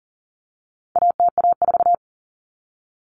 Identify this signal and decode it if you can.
Morse code, heard as a keyed tone, about 40 wpm, 700 Hz. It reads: ATA4
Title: Common Cut Numbers - 4-Digit 40wpm